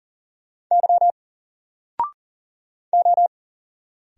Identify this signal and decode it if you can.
Morse code, heard as a keyed tone, about 40 wpm, 700 Hz. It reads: Y O
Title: Single Character (Letter:Number:Punctuation:Prosign) 40wpm